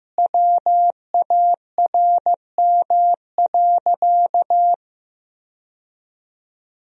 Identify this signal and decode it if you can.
Morse code, heard as a keyed tone, about 14 wpm, 700 Hz. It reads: WARM.
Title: Sentences from Top 1000 Words - Review Part 2 - 15wpm